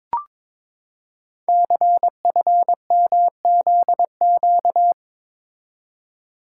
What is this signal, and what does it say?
Morse code, heard as a keyed tone, about 22 wpm, 700 Hz. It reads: CFMZQ
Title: Code Groups - 5 Letters - 22wpm